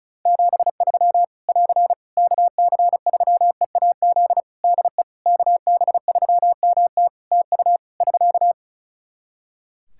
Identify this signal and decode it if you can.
Morse code, heard as a keyed tone, about 34 wpm, 700 Hz. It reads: 73 <AR> KC3EAZ DE KB3MT TU <SK>
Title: Practice QSO - Exchange 4 - Ending - 35wpm